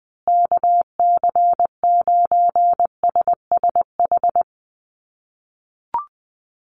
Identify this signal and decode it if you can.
Morse code, heard as a keyed tone, about 20 wpm, 700 Hz. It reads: KC9SSH
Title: USA Call Signs Repeated 2x 20wpm